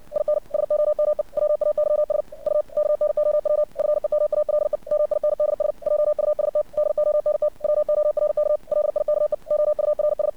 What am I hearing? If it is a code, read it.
Morse code, heard as a keyed tone, about 59 wpm, 600 Hz. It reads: IN SOME WAYS U KNOW FEMALE MINDS JUST DONT WORK LIKE OURS
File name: archived / CRSnet